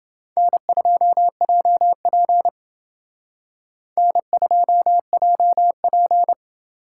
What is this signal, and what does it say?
Morse code, heard as a keyed tone, about 29 wpm, 700 Hz. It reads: N2JP N2JP